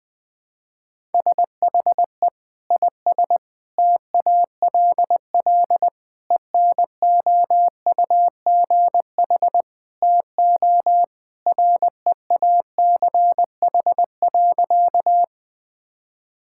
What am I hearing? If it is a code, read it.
Morse code, heard as a keyed tone, about 20 wpm, 700 Hz. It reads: SHE IS TALL ENOUGH TO REACH.
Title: Sentences with 2-Word Phrases - 20wpm